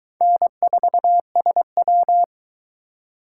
Morse code, heard as a keyed tone, about 23 wpm, 700 Hz. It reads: N4SW